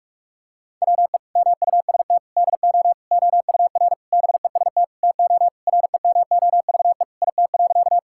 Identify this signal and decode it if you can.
Morse code, heard as a keyed tone, about 46 wpm, 700 Hz. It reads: WE MUST DO OUR BEST TO REMOVE IT.
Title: Sentences from Top 1000 Words - Review Part 5 - 45wpm